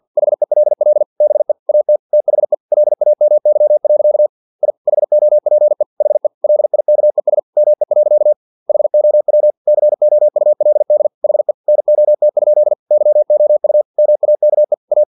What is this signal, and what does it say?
Morse code, heard as a keyed tone, about 50 wpm, 600 Hz. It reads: HELL BE AT THE LAMY. I HOPE HE LIKES ME. HOW COULD HE NOT? YOU MAKE A
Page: Never 49wpm